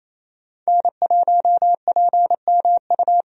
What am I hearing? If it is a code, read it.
Morse code, heard as a keyed tone, about 27 wpm, 700 Hz. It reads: N1PMU
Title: USA Call Signs - 5 Characters - 28wpm